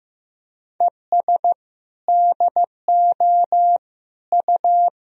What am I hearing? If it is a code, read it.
Morse code, heard as a keyed tone, about 15 wpm, 700 Hz. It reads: ES DO U
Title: Sentences from Top 100 Words and Abbreviations 15wpm